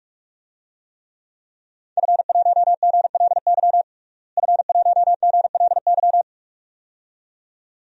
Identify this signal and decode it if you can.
Morse code, heard as a keyed tone, about 46 wpm, 700 Hz. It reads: F1GLY F1GLY
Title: DXCC-100 Call Signs - 5 Characters - Repeated 2x - 45wpm